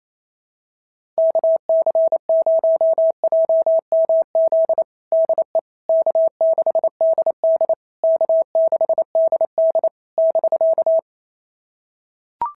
Morse code, heard as a keyed tone, about 27 wpm, 650 Hz. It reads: KC0JMZ DE K6DD K6DD <BK>
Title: SOTA QSO Practice 28wpm